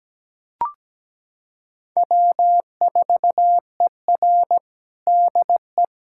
Morse code, heard as a keyed tone, about 17 wpm, 700 Hz. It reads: W4ER DE
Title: Practice QSO - Exchange 4 - Ending - 17wpm